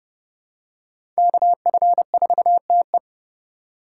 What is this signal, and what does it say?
Morse code, heard as a keyed tone, about 30 wpm, 700 Hz. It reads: KF4TE
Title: USA Call Signs - 5 Characters - 2x Character Spacing - 30wpm